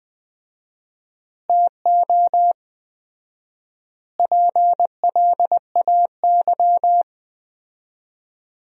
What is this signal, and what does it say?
Morse code, heard as a keyed tone, about 20 wpm, 700 Hz. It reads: TO PLAY
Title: 4x Word-Spacing - Sentences from Top 300 Words - 20wpm